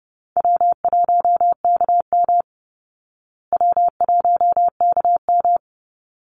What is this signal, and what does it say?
Morse code, heard as a keyed tone, about 30 wpm, 700 Hz. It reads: W1KM W1KM
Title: Call Signs Repeated 3x - 30wpm